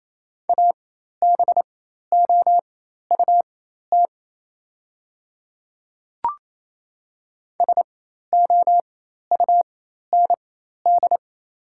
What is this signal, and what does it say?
Morse code, heard as a keyed tone, about 28 wpm, spaced as for 13 wpm, 700 Hz. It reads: ABOUT SOUND
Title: 4x Character Spacing - Top 200 Words - 28wpm